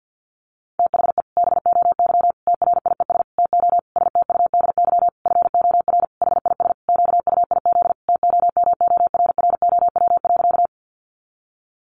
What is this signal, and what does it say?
Morse code, heard as a keyed tone, about 50 wpm, 700 Hz. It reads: THE BOY TRIES TO STUDY FOR HIS QUIZ TOMORROW.
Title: Additional Easy Phrases 50wpm